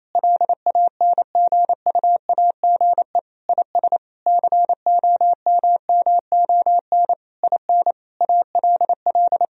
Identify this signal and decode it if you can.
Morse code, heard as a keyed tone, about 28 wpm, 690 Hz. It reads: LANGUAGE IS COMMON IN ALL
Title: Sentences from Top 1000 Words - Review Part 3 - 28wpm